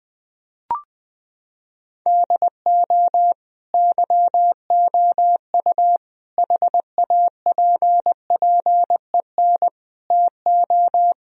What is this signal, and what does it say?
Morse code, heard as a keyed tone, about 20 wpm, 700 Hz. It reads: DO YOU HAPPEN TO
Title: Sentences from Top 1000 Words - Review Part 2 - 20wpm